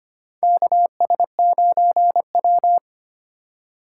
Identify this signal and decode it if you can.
Morse code, heard as a keyed tone, about 25 wpm, 700 Hz. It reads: KS9W
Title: K1USN SST QSO Practice - 25wpm